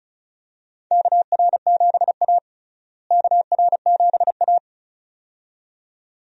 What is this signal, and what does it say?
Morse code, heard as a keyed tone, about 36 wpm, 700 Hz. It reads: KR7A KR7A
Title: USA Call Signs - 4 Characters - Repeated 2x - 35wpm